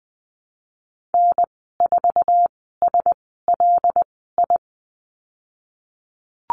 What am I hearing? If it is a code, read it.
Morse code, heard as a keyed tone, about 20 wpm, spaced as for 15 wpm, 700 Hz. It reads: N4SLI
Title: 082 Lesson 14g - Number 2 - Call Signs 20wpm